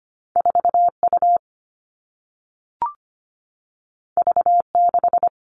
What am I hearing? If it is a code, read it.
Morse code, heard as a keyed tone, about 25 wpm, 700 Hz. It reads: 4 U V 6
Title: Two Challenging Letter-Number Combinations 25wpm